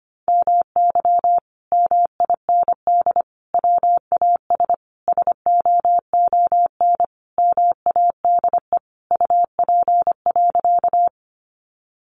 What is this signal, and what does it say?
Morse code, heard as a keyed tone, about 25 wpm, 700 Hz. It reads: MY MIND WAS SOON MADE UP.
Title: Sentences from Top 500 Words 25wpm